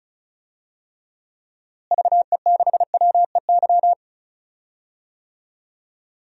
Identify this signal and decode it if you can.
Morse code, heard as a keyed tone, about 36 wpm, 700 Hz. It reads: VE6WEY
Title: DXCC-100 Call Signs - 6 Characters - Repeated 2x - 35wpm